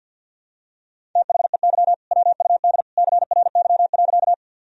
Morse code, heard as a keyed tone, about 50 wpm, 700 Hz. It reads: THEY WUD CRY.
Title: Sentences from Top 500 Words and Abbreviations 50wpm